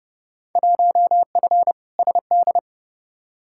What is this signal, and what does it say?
Morse code, heard as a keyed tone, about 32 wpm, 700 Hz. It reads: 1F SD
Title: Field Day Exchanges - 30wpm